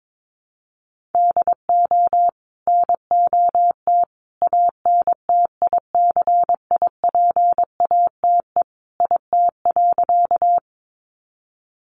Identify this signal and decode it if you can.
Morse code, heard as a keyed tone, about 22 wpm, 700 Hz. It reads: DO NOT ANTICIPATE IT.
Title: Sentences from Top 1500 Words 22wpm